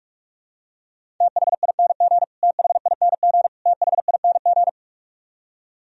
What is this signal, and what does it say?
Morse code, heard as a keyed tone, about 46 wpm, 700 Hz. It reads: THING THING THING
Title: Top 100 Words Repeated 45wpm